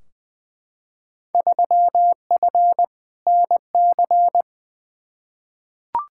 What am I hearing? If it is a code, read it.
Morse code, heard as a keyed tone, about 21 wpm, 700 Hz. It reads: 3F NC
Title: Field Day Exchanges - 20wpm